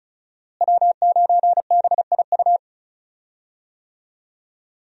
Morse code, heard as a keyed tone, about 36 wpm, 700 Hz. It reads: W9BIU
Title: USA Call Signs - 5 Characters - Repeated 3x - 35wpm